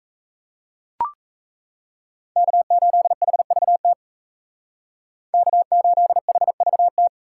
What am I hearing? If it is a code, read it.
Morse code, heard as a keyed tone, about 33 wpm, 700 Hz. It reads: K8HVT K8HVT
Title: Speed Racing - USA 1x3 Call Signs 28wpm